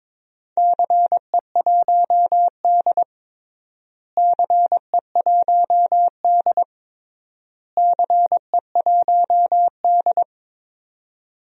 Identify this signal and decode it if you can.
Morse code, heard as a keyed tone, about 22 wpm, 700 Hz. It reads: CE1D CE1D CE1D
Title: DXCC-100 Call Signs - 4 Characters - Repeated 3x - 22wpm